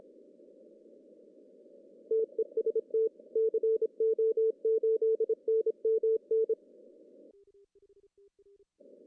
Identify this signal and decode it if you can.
Morse code, heard as a keyed tone, about 26 wpm, 440 Hz. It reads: TEST CO8NMN
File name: CW / cqww-cw-k5tr